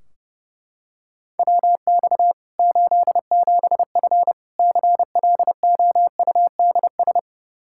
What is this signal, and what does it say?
Morse code, heard as a keyed tone, about 30 wpm, 700 Hz. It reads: WX 87F CLOUDS
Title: K1USN SST QSO Practice 30wpm